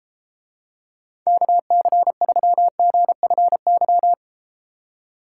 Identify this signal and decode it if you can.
Morse code, heard as a keyed tone, about 31 wpm, 700 Hz. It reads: KC3GFY